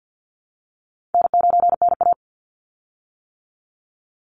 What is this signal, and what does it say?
Morse code, heard as a keyed tone, about 50 wpm, 700 Hz. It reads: N9NA
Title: Easy Call Signs - 50wpm